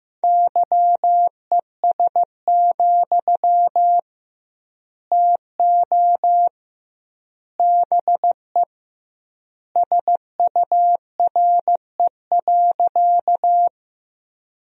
Morse code, heard as a keyed tone, about 15 wpm, 700 Hz. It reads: YES, TO BE SURE.
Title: 2x Word-Spacing - Sentences from Top 500 Words - 15wpm